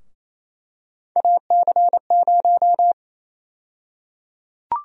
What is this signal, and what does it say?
Morse code, heard as a keyed tone, about 28 wpm, 700 Hz. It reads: AC0
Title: USA Call Sign Prefixes - Repeated 2x - 28wpm